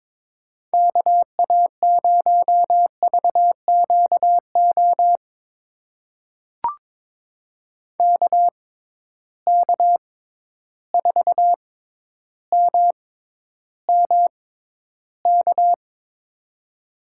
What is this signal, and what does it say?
Morse code, heard as a keyed tone, about 22 wpm, spaced as for 11 wpm, 700 Hz. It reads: KA0VQO KK4MMK
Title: USA Call Signs - 6x Character Spacing - 22wpm